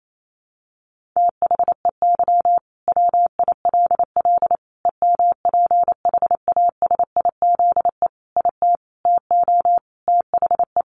Morse code, heard as a keyed tone, about 28 wpm, 700 Hz. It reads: THEY WILL EMPHASIZE IT TO THE
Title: Sentences from Top 1500 Words 28wpm